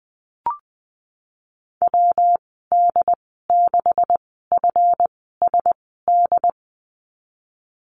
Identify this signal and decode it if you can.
Morse code, heard as a keyed tone, about 20 wpm, spaced as for 15 wpm, 700 Hz. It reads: WD6FSD
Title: 179 Lesson 29g - Slash - Call Signs 20wpm